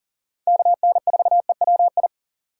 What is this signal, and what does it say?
Morse code, heard as a keyed tone, about 40 wpm, 690 Hz. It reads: KN4EWI
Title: K1USN SST QSO Practice - 40wpm